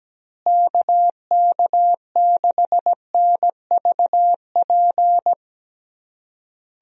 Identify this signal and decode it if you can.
Morse code, heard as a keyed tone, about 17 wpm, 700 Hz. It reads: KK6NVP
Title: USA Call Signs - 6 Characters - Repeated 3x - 17wpm